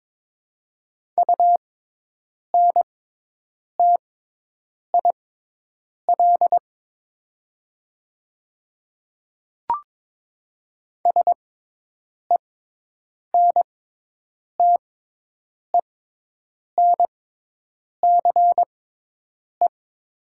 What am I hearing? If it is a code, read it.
Morse code, heard as a keyed tone, about 22 wpm, spaced as for 8 wpm, 700 Hz. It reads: UNTIL SENTENCE
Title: 6x Character Spacing - Top 300 Words - 22wpm